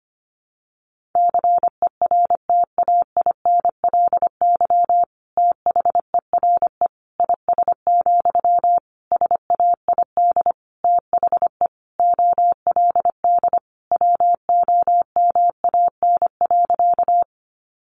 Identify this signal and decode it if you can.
Morse code, heard as a keyed tone, about 25 wpm, 700 Hz. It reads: CERTAINLY THERE IS, SAID THE OLD WOMAN.
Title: Sentences from Top 1100 Words 25wpm